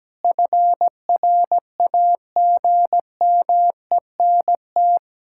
Morse code, heard as a keyed tone, about 17 wpm, 700 Hz. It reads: FRAGMENT
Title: Pangram Challenge v2 - 17wpm